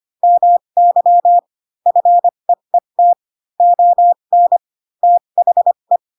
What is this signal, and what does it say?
Morse code, heard as a keyed tone, about 25 wpm, spaced as for 22 wpm, 700 Hz. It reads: MY FEET ON THE
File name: Pinocchio / Pinocchio ch7 18 25 ik1hge